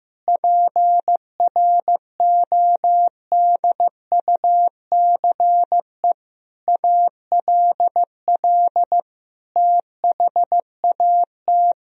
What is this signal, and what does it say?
Morse code, heard as a keyed tone, about 16 wpm, 700 Hz. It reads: PRODUCE ALL THAT
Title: Sentences from Top 1000 Words - Review Part 3 - 15wpm